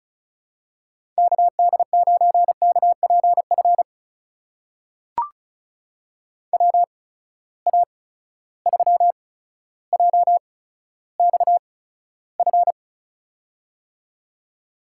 Morse code, heard as a keyed tone, about 36 wpm, spaced as for 15 wpm, 700 Hz. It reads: KD9KPF WA3JXF
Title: USA Call Signs - 6 Characters - 8x Character Spacing - 35wpm